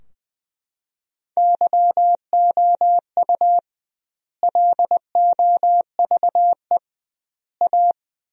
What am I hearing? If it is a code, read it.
Morse code, heard as a keyed tone, about 20 wpm, 700 Hz. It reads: YOU LOVE A
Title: 2x Word-Spacing - Sentences from Top 500 Words - 20wpm